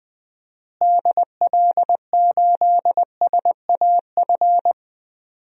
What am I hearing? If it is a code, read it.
Morse code, heard as a keyed tone, about 20 wpm, 700 Hz. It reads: DL8SAF
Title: DXCC-100 Call Signs - 6 Characters - Repeated 3x - 20wpm